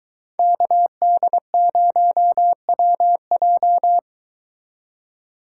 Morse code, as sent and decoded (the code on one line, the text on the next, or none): -.- -.. ----- .-- .---
KD0WJ